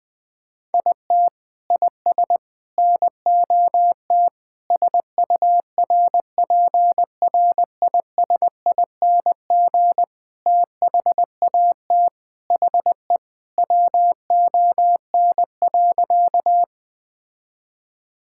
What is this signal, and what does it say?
Morse code, heard as a keyed tone, about 21 wpm, 700 Hz. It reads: IT IS NOT SURPRISING THAT HE WON.
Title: Sentences with 5-Word Phrases - 20wpm